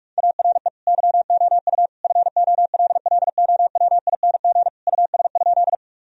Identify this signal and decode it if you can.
Morse code, heard as a keyed tone, about 42 wpm, 700 Hz. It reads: ARE YOU FOLLOWING US?
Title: Sentences from Top 1000 Words - Review Part 4 - 45wpm